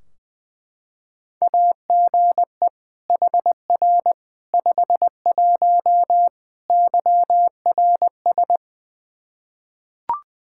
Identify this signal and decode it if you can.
Morse code, heard as a keyed tone, about 21 wpm, 700 Hz. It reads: AGE HR 51 YRS
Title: Practice QSO - Exchange 3 - More Info - 20wpm